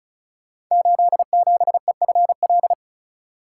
Morse code, heard as a keyed tone, about 36 wpm, 700 Hz. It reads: 87EFL
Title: Code Groups - 5 Letter:Numbers - 35wpm